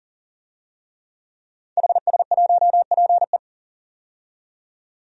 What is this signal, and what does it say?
Morse code, heard as a keyed tone, about 40 wpm, 700 Hz. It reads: HS1PE
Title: DXCC-100 Call Signs - 5 Characters - 40wpm